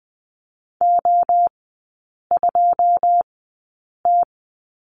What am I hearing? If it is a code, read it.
Morse code, heard as a keyed tone, about 20 wpm, spaced as for 8 wpm, 700 Hz. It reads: O2T
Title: 127 Lesson 21e - Number 6 - Three Characters 20wpm